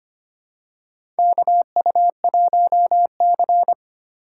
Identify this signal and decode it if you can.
Morse code, heard as a keyed tone, about 25 wpm, 700 Hz. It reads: KU1C
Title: USA Call Signs - 4x Character Spacing - 25wpm